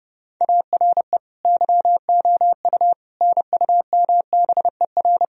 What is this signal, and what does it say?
Morse code, heard as a keyed tone, about 30 wpm, 700 Hz. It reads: ARE YOU NUMBER
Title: Sentences from Top 1000 Words - Review Part 1 - 30wpm